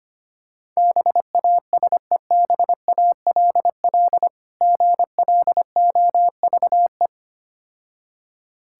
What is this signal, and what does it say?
Morse code, heard as a keyed tone, about 26 wpm, 700 Hz. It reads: BASEBALL GLOVE
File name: practice / 25wpm / Difficult Words - Speed-Racing with Hints - 25wpm